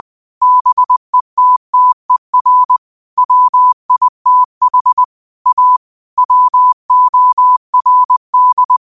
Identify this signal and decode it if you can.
Morse code, heard as a keyed tone, about 20 wpm, 1000 Hz. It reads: BETTER WITH A WORD
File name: qotdfiles / 20wpm